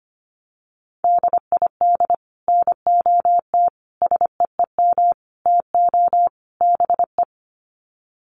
Morse code, heard as a keyed tone, about 25 wpm, 700 Hz. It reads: DID NOT SEEM TO BE
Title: Sets of 5 Words Encore 25wpm